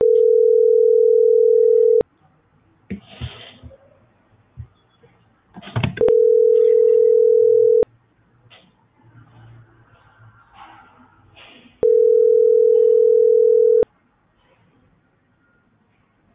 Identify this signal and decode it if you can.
Ringback tone (North America), 440 + 480 Hz.